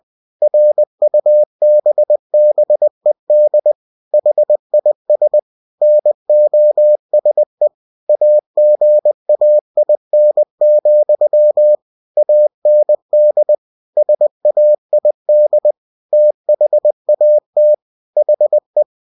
Morse code, heard as a keyed tone, about 20 wpm, 600 Hz. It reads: RUBBED HIS NOSE AGAIN, AND SAID THAT HE
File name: cw-books / winnie-the-pooh / winnie-20wpm-page-0035